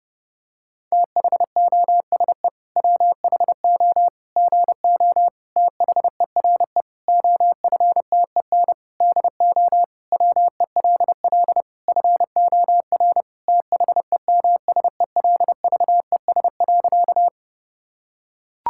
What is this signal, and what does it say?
Morse code, heard as a keyed tone, about 32 wpm, 700 Hz. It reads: THOSE WHO GO THERE OFTEN DO WELL FOR THEMSELVES.
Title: Sentences from Top 1000 Words - Review Part 5 - 30wpm